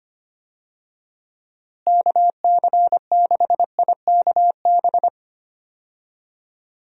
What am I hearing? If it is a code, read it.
Morse code, heard as a keyed tone, about 25 wpm, 700 Hz. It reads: KC6IKB
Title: USA Call Signs - 6 Characters - Repeated 3x - 25wpm